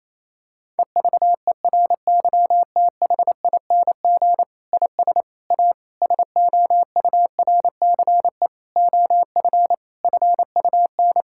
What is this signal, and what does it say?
Morse code, heard as a keyed tone, about 27 wpm, 700 Hz. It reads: EVERYTHING IS A SOURCE OF FUN
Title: Sentences from Top 1000 Words - Review Part 5 - 28wpm